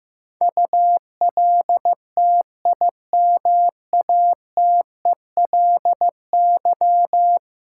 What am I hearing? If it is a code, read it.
Morse code, heard as a keyed tone, about 15 wpm, 700 Hz. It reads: ULTIMATELY
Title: Sentences from Top 1200 Words - New Word Drill - 15wpm